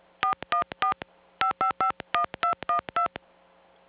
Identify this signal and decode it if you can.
DTMF, heard as telephone touch tones, tones of about 110 ms. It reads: 414 5551212